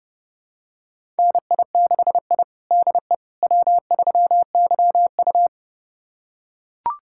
Morse code, heard as a keyed tone, about 32 wpm, 700 Hz. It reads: NI6I DE W3YU
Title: USA Callsign DE Callsign - 4 Characters - 30wpm